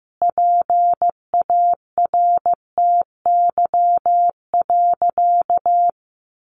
Morse code, heard as a keyed tone, about 15 wpm, 700 Hz. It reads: PARTY.
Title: Sentences from Top 1500 Words 15wpm